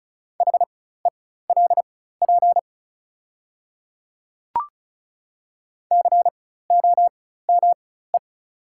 Morse code, heard as a keyed tone, about 36 wpm, spaced as for 16 wpm, 700 Hz. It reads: HELP COME
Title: 4x Character Spacing - Top 200 Words - 35wpm